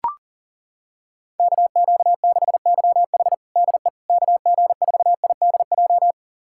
Morse code, heard as a keyed tone, about 40 wpm, 700 Hz. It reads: KQ6YH DE KG4IDJ
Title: USA Callsign DE Callsign 40wpm